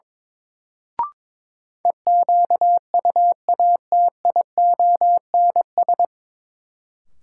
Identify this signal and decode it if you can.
Morse code, heard as a keyed tone, about 22 wpm, 700 Hz. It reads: EQUATIONS
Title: Sets of 5 Words - New Word Drill - 22wpm